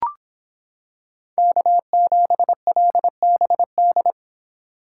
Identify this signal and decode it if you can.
Morse code, heard as a keyed tone, about 25 wpm, 700 Hz. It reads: K7LBD